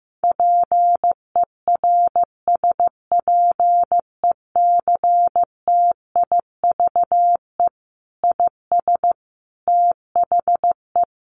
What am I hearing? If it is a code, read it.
Morse code, heard as a keyed tone, about 15 wpm, 700 Hz. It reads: PERSPECTIVE IS THE
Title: Sentences from Top 1200 Words 15wpm